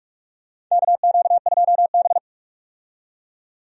Morse code, heard as a keyed tone, about 46 wpm, 700 Hz. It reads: KQ2B
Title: USA Call Signs - 4 Characters - 45wpm